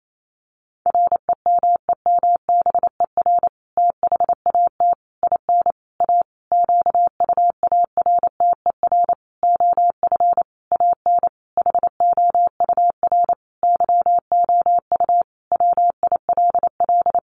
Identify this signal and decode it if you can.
Morse code, heard as a keyed tone, about 28 wpm, 700 Hz. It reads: REMEMBER THAT IN A QUARTER OF AN HOUR YOU WILL
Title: Sentences from Top 1300 Words 28wpm